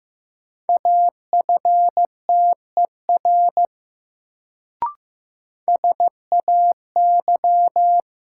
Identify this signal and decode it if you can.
Morse code, heard as a keyed tone, about 15 wpm, 700 Hz. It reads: AFTER SAY
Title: Sentences from Top 200 Words - New Word Drill - 15wpm